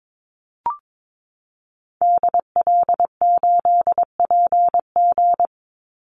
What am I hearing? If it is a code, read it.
Morse code, heard as a keyed tone, about 22 wpm, 700 Hz. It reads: DL8PG
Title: Call Signs - 22wpm